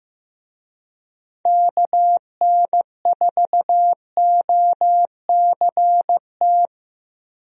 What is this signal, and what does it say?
Morse code, heard as a keyed tone, about 15 wpm, 700 Hz. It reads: KN4OCT
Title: USA Call Signs - 6 Characters - Repeated 3x - 15wpm